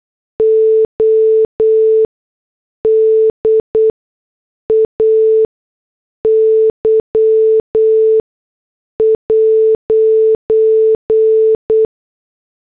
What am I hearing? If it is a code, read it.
Morse code, heard as a keyed tone, about 8 wpm, spaced as for 6 wpm, 440 Hz. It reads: ODAY'